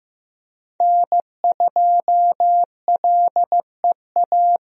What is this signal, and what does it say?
Morse code, heard as a keyed tone, about 15 wpm, 700 Hz. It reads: N2LEA
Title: USA Call Signs - 5 Characters - Repeated 2x - 15wpm